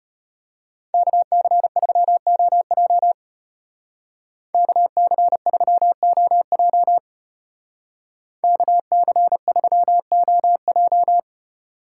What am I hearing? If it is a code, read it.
Morse code, heard as a keyed tone, about 31 wpm, 700 Hz. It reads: KC3OJ KC3OJ KC3OJ
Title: Speed Racing - USA 2x2 Call Signs 25wpm